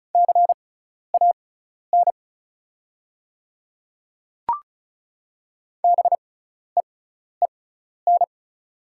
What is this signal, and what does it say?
Morse code, heard as a keyed tone, about 36 wpm, spaced as for 12 wpm, 700 Hz. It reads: CAN BEEN
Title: 6x Character Spacing - Top 100 Words - 35wpm